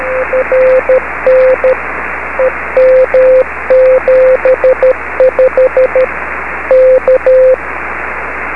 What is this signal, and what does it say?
Morse code, heard as a keyed tone, about 13 wpm, 520 Hz, with radio noise. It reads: RN EM75 K
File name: beacons / k4kwk-b